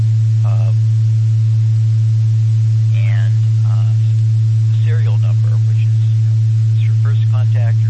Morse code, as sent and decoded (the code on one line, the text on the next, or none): none